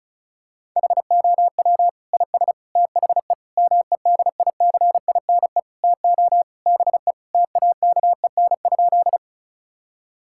Morse code, heard as a keyed tone, about 34 wpm, 700 Hz. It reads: HOW IS THE MEDICINE TO BE TAKEN?
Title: Sentences from Top 1000 Words - Review Part 6 - 35wpm